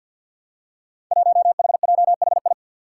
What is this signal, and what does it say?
Morse code, heard as a keyed tone, about 50 wpm, 700 Hz. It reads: 1HJHI
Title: Five Challenging Letter-Number Combinations 50wpm